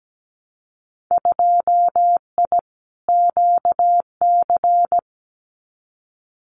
Morse code, heard as a keyed tone, about 17 wpm, 700 Hz. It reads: 2I QC
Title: Winter Field Day Exchanges 17wpm